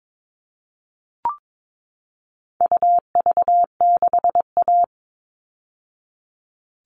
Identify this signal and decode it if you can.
Morse code, heard as a keyed tone, about 22 wpm, 700 Hz. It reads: UV6A
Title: All Cut Numbers - 4-Digit 22wpm